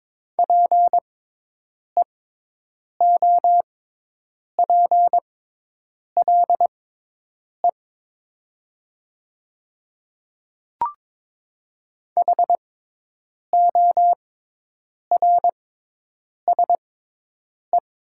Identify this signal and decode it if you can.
Morse code, heard as a keyed tone, about 22 wpm, spaced as for 8 wpm, 700 Hz. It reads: PEOPLE HORSE
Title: 6x Character Spacing - Top 300 Words - 22wpm